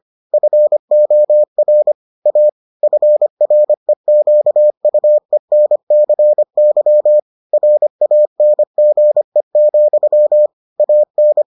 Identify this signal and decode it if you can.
Morse code, heard as a keyed tone, about 25 wpm, 600 Hz. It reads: FOR A FREQUENCY RANGE, AN